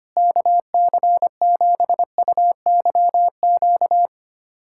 Morse code, heard as a keyed tone, about 25 wpm, 700 Hz. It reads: KC7UYQ
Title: USA Callsign DE Callsign - 6 Characters - 25wpm